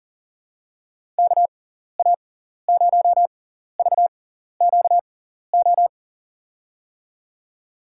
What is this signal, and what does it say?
Morse code, heard as a keyed tone, about 37 wpm, spaced as for 14 wpm, 700 Hz. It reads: KA0VQO
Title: USA Call Signs - 6 Characters - 6x Character Spacing - 40wpm